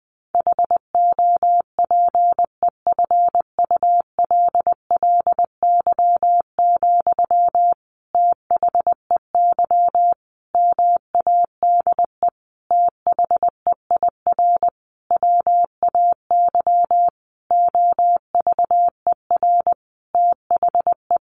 Morse code, heard as a keyed tone, about 20 wpm, 700 Hz. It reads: HOPEFULLY, THEY MADE THEIR WAY OVER THE
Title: Sentences from Top 1500 Words 20wpm